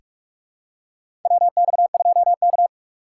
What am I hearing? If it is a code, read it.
Morse code, heard as a keyed tone, about 46 wpm, 700 Hz. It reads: WX2K
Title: USA Call Signs - 4 Characters - 4x Character Spacing - 45wpm